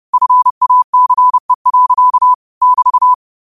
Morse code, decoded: FACE. <BT>

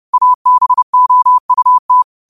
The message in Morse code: ABOUT

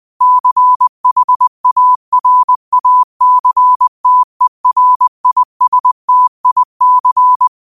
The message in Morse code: CHARACTERISTIC

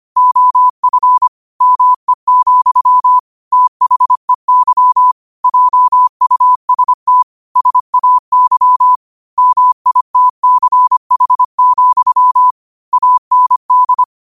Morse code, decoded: OF ME, THEY JUST SAY MITCH, AND

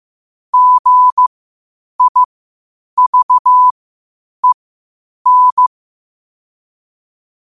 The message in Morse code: GIVEN

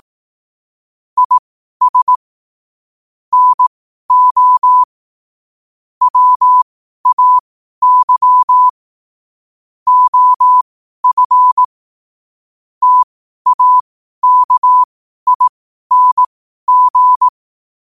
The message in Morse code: IS NO WAY OF TAKING

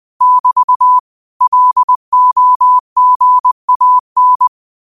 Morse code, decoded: <BT> LOGAN

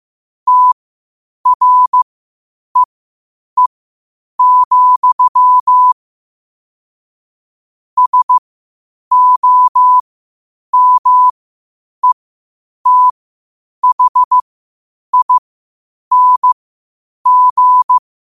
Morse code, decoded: TREE, SOMETHING

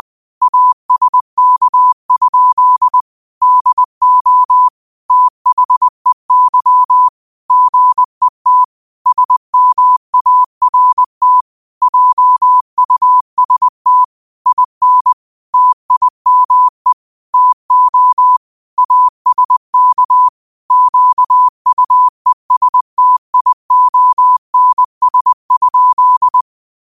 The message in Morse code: ASK? DO THEY GET SMART JUST IN TIME TO ASK QUESTIONS?

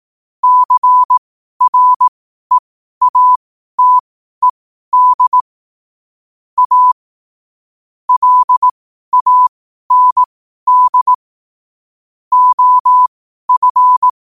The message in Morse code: CREATED A LAND OF

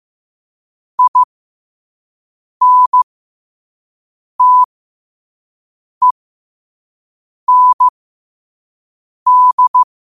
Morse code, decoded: INTEND

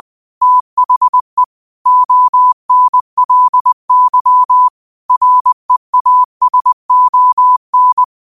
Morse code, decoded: THE ONLY REASON

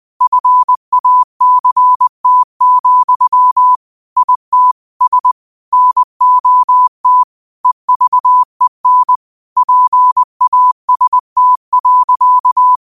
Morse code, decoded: FACT, ITS NOT EVEN PAST.

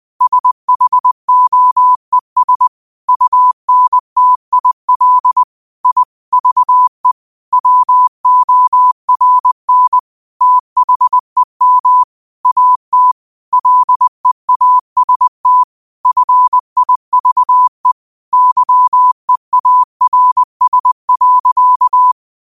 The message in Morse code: SHOES UNTIL I VE WORN THEM AT LEAST FIVE YEARS.

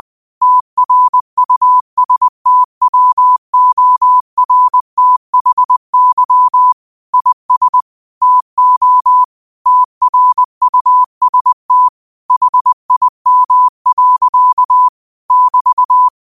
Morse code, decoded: TRUSTWORTHY IS TO TRUST HIM. <BT>

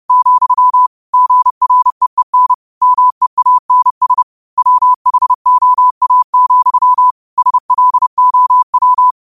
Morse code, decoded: , GREEN MEANS WHOA, SLOW

